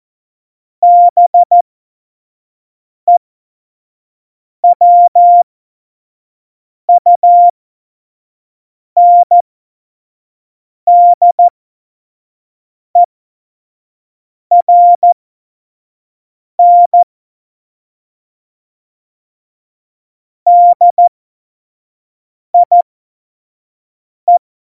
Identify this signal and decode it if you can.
Morse code, heard as a keyed tone, about 14 wpm, spaced as for 5 wpm, 700 Hz. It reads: BEWUNDERN DIE
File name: Morsekurs / Witze / Witz 08. Mai 2018, 25 F70